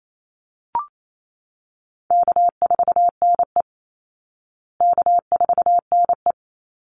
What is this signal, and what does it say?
Morse code, heard as a keyed tone, about 28 wpm, 700 Hz. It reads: K4NE K4NE
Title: Call Signs Repeated 3x - 28wpm